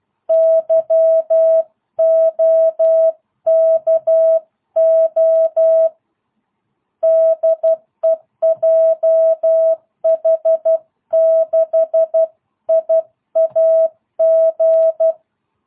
Morse code, heard as a keyed tone, about 12 wpm, 660 Hz. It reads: YOKO DEJH6IAG